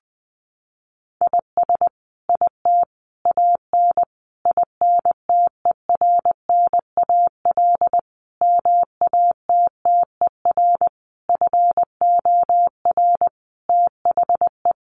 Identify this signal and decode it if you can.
Morse code, heard as a keyed tone, about 20 wpm, 700 Hz. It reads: IS IT AN INTERNAL MATTER FOR THE
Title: Sentences from Top 1400 Words 20wpm